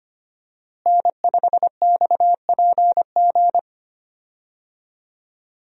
Morse code, heard as a keyed tone, about 25 wpm, 700 Hz. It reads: N5XPG
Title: USA Call Signs - 5 Characters - Repeated 3x - 25wpm